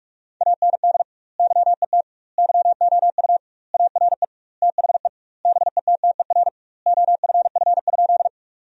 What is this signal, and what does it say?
Morse code, heard as a keyed tone, about 46 wpm, 700 Hz. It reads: AND YET YOU ARE THE BETTER OFF?
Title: 2x Word-Spacing - Sentences from Top 500 Words - 45wpm